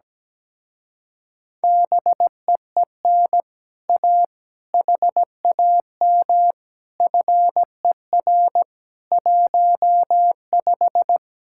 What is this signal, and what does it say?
Morse code, heard as a keyed tone, about 17 wpm, 700 Hz. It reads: BEEN A HAM FER 15
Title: Speed-Racing Common QSO Phrases - 17wpm